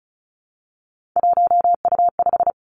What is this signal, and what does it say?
Morse code, heard as a keyed tone, about 36 wpm, 700 Hz. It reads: 1U5
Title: Three Challenging Letter-Number Combinations 35wpm